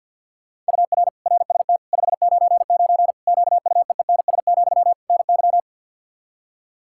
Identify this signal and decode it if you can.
Morse code, heard as a keyed tone, about 46 wpm, 700 Hz. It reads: UR RST 599 QUEENS, NY
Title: Practice QSO - Exchange 1 - Beginning - 50wpm